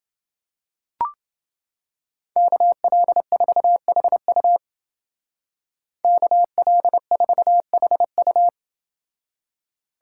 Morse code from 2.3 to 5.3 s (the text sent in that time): KL4HU